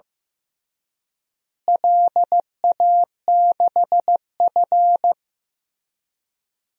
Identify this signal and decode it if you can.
Morse code, heard as a keyed tone, about 15 wpm, 700 Hz. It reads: LA6F
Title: DXCC-100 Call Signs - 4 Characters - Repeated 3x - 15wpm